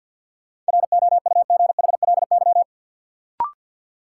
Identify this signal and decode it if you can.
Morse code, heard as a keyed tone, about 53 wpm, 700 Hz. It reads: ROUGHLY
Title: Sentences from Top 1500 Words - New Word Drill - 50wpm